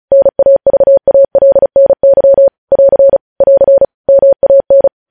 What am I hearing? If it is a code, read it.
Morse code, heard as a keyed tone, about 36 wpm, 550 Hz. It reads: NAVALNY <AR> <AR> MAN
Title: News Headlines at 35